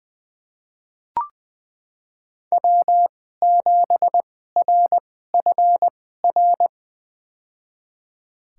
Morse code, heard as a keyed tone, about 20 wpm, spaced as for 14 wpm, 700 Hz. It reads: W7RFR
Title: USA Call Signs - 5 Characters - 2x Character Spacing - 20wpm